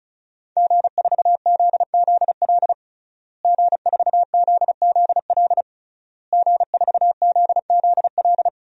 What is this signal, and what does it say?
Morse code, heard as a keyed tone, about 34 wpm, 700 Hz. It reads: G4ZZL G4ZZL G4ZZL
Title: DXCC-100 Call Signs - 5 Characters - Repeated 3x - 35wpm